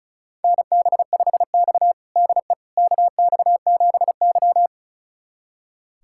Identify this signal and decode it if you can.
Morse code, heard as a keyed tone, about 36 wpm, 700 Hz. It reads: NB5X DE KX7Y
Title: USA Callsign DE Callsign - 4 Characters - 35wpm